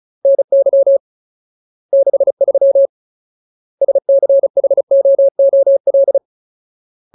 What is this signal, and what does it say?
Morse code, heard as a keyed tone, about 34 wpm, 550 Hz. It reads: NY 63 SCHOOL
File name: Ultimate Word List / 35WPM Ultimate